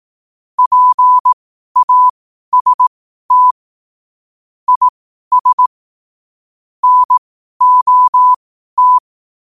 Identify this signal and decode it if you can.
Morse code, heard as a keyed tone, about 18 wpm, spaced as for 12 wpm, 1000 Hz. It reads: PAST IS NOT
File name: qotdfiles / 12wpm